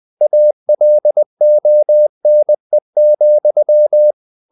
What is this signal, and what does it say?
Morse code, heard as a keyed tone, about 20 wpm, 600 Hz. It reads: ALONE,
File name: the old man and the sea / the old man and the sea part 1 20wpm